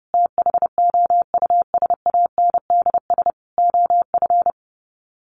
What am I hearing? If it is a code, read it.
Morse code, heard as a keyed tone, about 30 wpm, 700 Hz. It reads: THOUSANDS OF
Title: Sets of 4 Words 30wpm